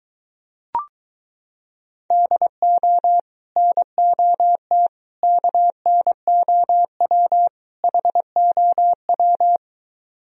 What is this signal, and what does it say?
Morse code, heard as a keyed tone, about 23 wpm, 700 Hz. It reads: DO NOT KNOW HOW